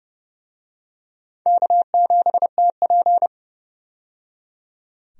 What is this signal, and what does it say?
Morse code, heard as a keyed tone, about 30 wpm, 700 Hz. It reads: K7TP
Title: USA Call Signs - 4 Characters - 30wpm